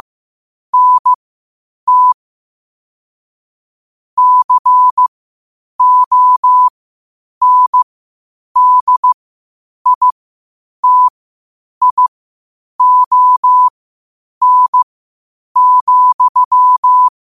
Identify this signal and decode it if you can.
Morse code, heard as a keyed tone, about 15 wpm, spaced as for 8 wpm, 1000 Hz. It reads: NT CONDITION,
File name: qotdfiles / 8wpm